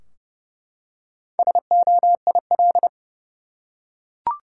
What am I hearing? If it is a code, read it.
Morse code, heard as a keyed tone, about 30 wpm, 700 Hz. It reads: SOIL
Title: Four Letter Words - 30wpm